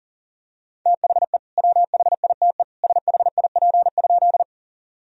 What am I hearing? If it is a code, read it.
Morse code, heard as a keyed tone, about 37 wpm, 700 Hz. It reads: THE WHITE SHIP?
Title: Sentences from Top 1000 Words - Review Part 2 - 40wpm